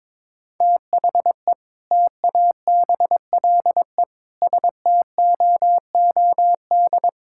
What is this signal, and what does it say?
Morse code, heard as a keyed tone, about 21 wpm, 700 Hz. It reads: THE TABLE STOOD
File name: practice / 22wpm / Sentences with 4-Word Phrases - 22wpm